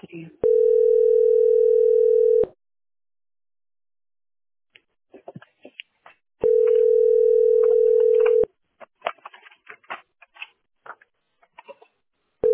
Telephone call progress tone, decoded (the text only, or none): ringback tone